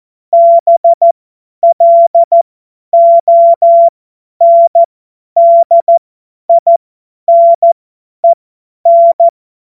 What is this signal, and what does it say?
Morse code, heard as a keyed tone, about 14 wpm, spaced as for 10 wpm, 680 Hz. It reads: BLONDINEN
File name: Morsekurs / Witze / Witz 13. April 2021, 50 F70